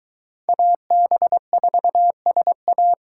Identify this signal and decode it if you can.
Morse code, heard as a keyed tone, about 23 wpm, 700 Hz. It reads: AB4SA